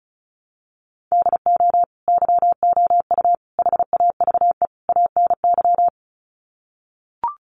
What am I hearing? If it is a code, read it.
Morse code, heard as a keyed tone, about 34 wpm, 700 Hz. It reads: DO YOU HAVE ANY
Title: Sets of 4 Words 35wpm